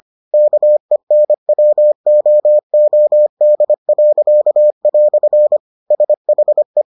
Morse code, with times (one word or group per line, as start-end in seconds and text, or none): 0.3-5.6 KENWOOD."
5.9-6.8 SHE